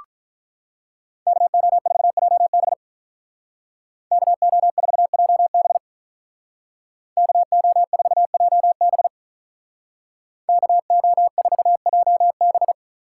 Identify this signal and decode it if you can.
Morse code, heard as a keyed tone, about 46 wpm, 700 Hz. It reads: KO4JB KO4JB KO4JB KO4JB